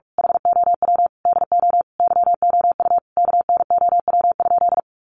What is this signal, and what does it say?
Morse code, heard as a keyed tone, about 44 wpm, 700 Hz. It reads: HOW DO YOU KNOW?
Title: Sentences from Top 100 Words 45wpm